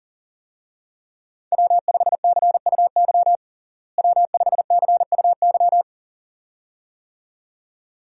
Morse code, heard as a keyed tone, about 37 wpm, 700 Hz. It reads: W5CUY W5CUY
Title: USA Call Signs - 5 Characters - Repeated 2x - 40wpm